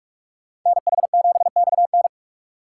Morse code, heard as a keyed tone, about 67 wpm, spaced as for 55 wpm, 700 Hz. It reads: NH7XN
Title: Speed Racing - USA 2x2 Call Signs 45wpm